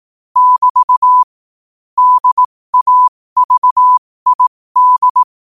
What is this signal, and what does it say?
Morse code, heard as a keyed tone, about 18 wpm, spaced as for 15 wpm, 1000 Hz. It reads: <BT> DAVID
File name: qotdfiles / 15wpm